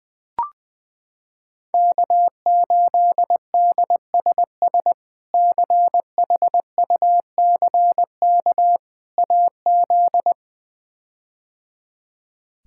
Morse code, heard as a keyed tone, about 20 wpm, 700 Hz. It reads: K8DSS CHUCK AZ
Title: K1USN SST QSO Practice - 20wpm